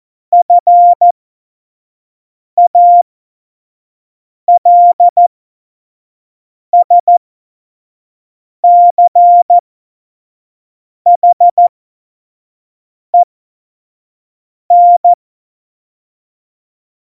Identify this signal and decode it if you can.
Morse code, heard as a keyed tone, about 14 wpm, spaced as for 5 wpm, 700 Hz. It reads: FALSCHEN
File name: Morsekurs / Witze / Witz 25. Juni 2019, 25 F70